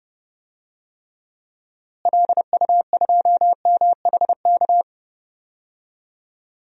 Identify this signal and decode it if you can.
Morse code, heard as a keyed tone, about 32 wpm, 700 Hz. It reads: LU2MHK
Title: DXCC-100 Call Signs - 6 Characters - Repeated 3x - 30wpm